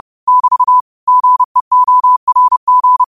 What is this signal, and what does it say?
Morse code, heard as a keyed tone, about 30 wpm, 1000 Hz. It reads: <BT> GEORG
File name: qotdfiles / 30wpm